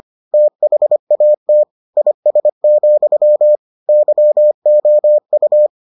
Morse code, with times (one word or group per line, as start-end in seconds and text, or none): 0.3-1.6 THAT
2.0-3.6 IS,
3.9-5.7 YOU